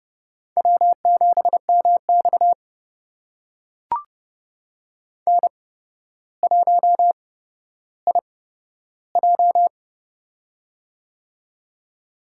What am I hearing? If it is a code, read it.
Morse code, heard as a keyed tone, about 30 wpm, spaced as for 13 wpm, 700 Hz. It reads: W7MX N1IJ
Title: USA Call Signs - 4 Characters - 8x Character Spacing - 30wpm